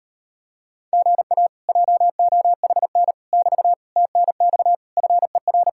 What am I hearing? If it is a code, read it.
Morse code, heard as a keyed tone, about 33 wpm, 700 Hz. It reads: GA JOHN <BT> TNX FER